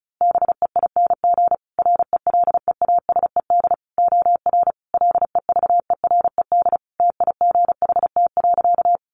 Morse code, heard as a keyed tone, about 34 wpm, 700 Hz. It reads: BEING RELEASED OR LEVERED TIGHT.
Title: A Sound Of Thunder 35wpm